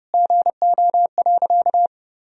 Morse code, heard as a keyed tone, about 30 wpm, 700 Hz. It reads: GO.
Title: Sentences from Top 400 Words and Abbreviations 30wpm